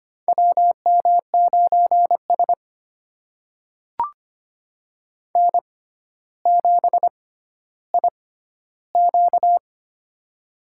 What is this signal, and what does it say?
Morse code, heard as a keyed tone, about 25 wpm, spaced as for 13 wpm, 700 Hz. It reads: WM9S N7IQ
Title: USA Call Signs - 4 Characters - 6x Character Spacing - 25wpm